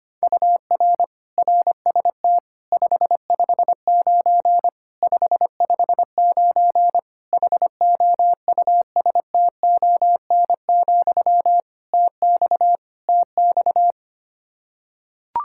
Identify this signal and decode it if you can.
Morse code, heard as a keyed tone, about 25 wpm, 700 Hz. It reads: UR RST 559 559 HOUSTON, TX TX
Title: K1USN SST QSO Practice 25wpm